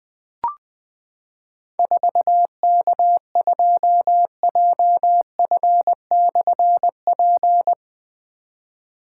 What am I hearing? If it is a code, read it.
Morse code, heard as a keyed tone, about 20 wpm, 700 Hz. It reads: 4K2JF/P
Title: Difficult Call Signs 20wpm